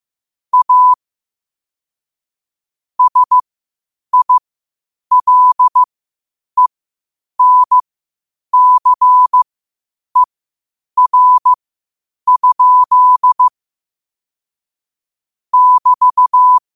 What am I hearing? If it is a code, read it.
Morse code, heard as a keyed tone, about 15 wpm, spaced as for 8 wpm, 1000 Hz. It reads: A SILENCER? <BT>
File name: 8wpm